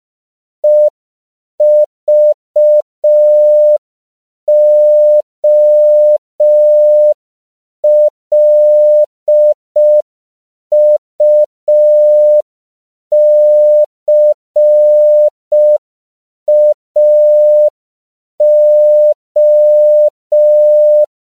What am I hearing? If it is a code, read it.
Morse code, heard as a keyed tone, about 5 wpm, 600 Hz. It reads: EVOLUCAO
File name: CW / 5PPM